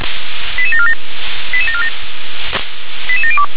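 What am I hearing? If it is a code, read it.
Five-tone selective call (ZVEI), heard as quick runs of five tones, tones of about 70 ms. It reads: 88747 88747 88702
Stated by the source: ZVEI five-tone selective call 88154 received via SDR.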